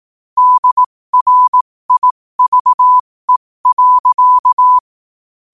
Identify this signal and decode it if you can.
Morse code, heard as a keyed tone, about 18 wpm, spaced as for 15 wpm, 1000 Hz. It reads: DRIVE.
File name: qotdfiles / 15wpm